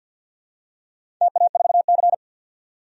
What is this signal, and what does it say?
Morse code, heard as a keyed tone, about 51 wpm, 700 Hz. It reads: TA4C
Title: DXCC-100 Call Signs - 4 Characters - 50wpm